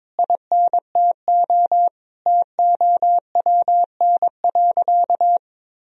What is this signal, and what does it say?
Morse code, heard as a keyed tone, about 22 wpm, 700 Hz. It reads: INTO TOWN.
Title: Sentences from Top 1000 Words - Review Part 2 - 22wpm